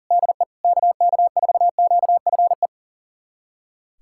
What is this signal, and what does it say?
Morse code, heard as a keyed tone, about 40 wpm, 700 Hz. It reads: DE KK4QFE
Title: USA Callsign DE Callsign - 6 Characters - 40wpm